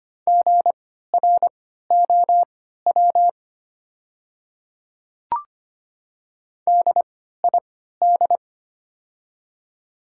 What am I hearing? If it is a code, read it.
Morse code, heard as a keyed tone, about 25 wpm, spaced as for 14 wpm, 700 Hz. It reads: GROW DID
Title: 3x Character Spacing - Top 300 Words - 25wpm